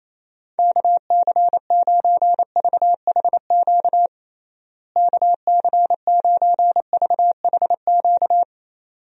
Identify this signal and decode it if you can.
Morse code, heard as a keyed tone, about 28 wpm, 700 Hz. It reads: KC9VHQ KC9VHQ
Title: USA Call Signs - 6 Characters - Repeated 2x - 28wpm